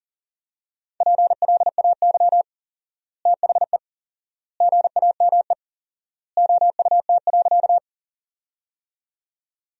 Morse code, heard as a keyed tone, about 40 wpm, 700 Hz. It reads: PLAY THE GAME OUT.
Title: 4x Word-Spacing - Sentences from Top 500 Words - 40wpm